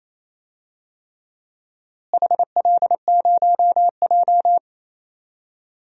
Morse code, heard as a keyed tone, about 28 wpm, 700 Hz. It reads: HL0J
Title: DXCC-100 Call Signs - 4 Characters - Repeated 3x - 28wpm